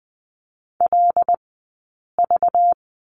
Morse code, heard as a keyed tone, about 20 wpm, 700 Hz. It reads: L V
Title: 190 Lesson 31d - Letter v - Two Characters 20wpm